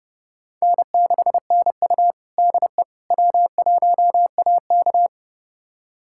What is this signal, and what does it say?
Morse code, heard as a keyed tone, about 30 wpm, 700 Hz. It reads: N6NU DE W1AK
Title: USA Callsign DE Callsign - 4 Characters - 30wpm